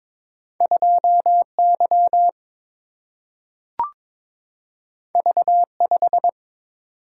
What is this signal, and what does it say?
Morse code, heard as a keyed tone, about 22 wpm, 700 Hz. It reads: 2 Y V 5
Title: Single-Letter Single-Number 22wpm